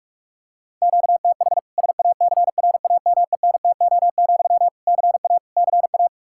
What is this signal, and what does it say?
Morse code, heard as a keyed tone, about 46 wpm, 700 Hz. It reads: QTH SACRAMENTO, CA CA
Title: Practice QSO - Exchange 1 - Beginning - 45wpm